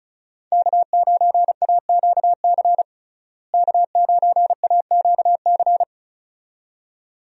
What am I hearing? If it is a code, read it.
Morse code, heard as a keyed tone, about 34 wpm, 700 Hz. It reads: K9AQC K9AQC
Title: USA Call Signs - 5 Characters - Repeated 2x - 35wpm